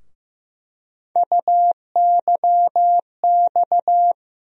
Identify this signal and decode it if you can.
Morse code, heard as a keyed tone, about 15 wpm, 700 Hz. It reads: UYX
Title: USA Call Sign Postfixes - 3 Letters 15wpm